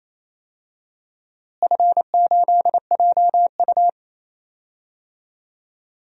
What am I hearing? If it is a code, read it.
Morse code, heard as a keyed tone, about 28 wpm, 700 Hz. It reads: F8JU
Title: DXCC-100 Call Signs - 4 Characters - Repeated 2x - 28wpm